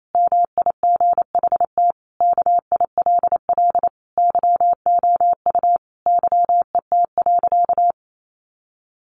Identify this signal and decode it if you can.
Morse code, heard as a keyed tone, about 28 wpm, 700 Hz. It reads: MIGHT KILL YOU YET.
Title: A Sound Of Thunder 28wpm